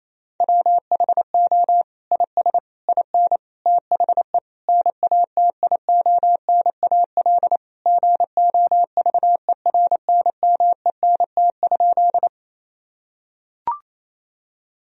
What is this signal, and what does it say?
Morse code, heard as a keyed tone, about 28 wpm, 700 Hz. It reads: WHO IS IN THE NATIONAL GOVERNMENT?
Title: Sentences from Top 1000 Words - Review Part 4 - 28wpm